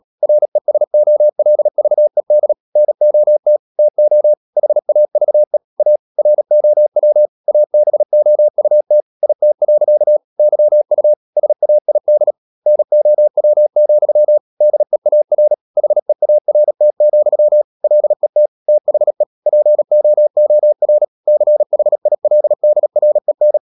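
Morse code, heard as a keyed tone, about 37 wpm, 600 Hz. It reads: RESOLVED NOT TO HAVE A ROW ABOUT IT. YU SAID NOW, DEAR HEART, LET THE POOR CHILDREN
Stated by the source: Morse code practice at 35–39 wpm sending English text of Ken Follett's novel Never, from CW med Gnister.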